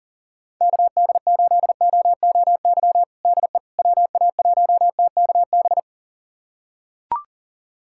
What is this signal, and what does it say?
Morse code, heard as a keyed tone, about 40 wpm, 700 Hz. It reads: KD8OOY DE WA1TKB
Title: USA Callsign DE Callsign - 6 Characters - 40wpm